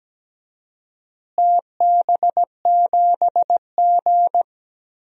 Morse code, heard as a keyed tone, about 17 wpm, 700 Hz. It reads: TB7G
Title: DXCC-100 Call Signs - 4 Characters - Repeated 2x - 17wpm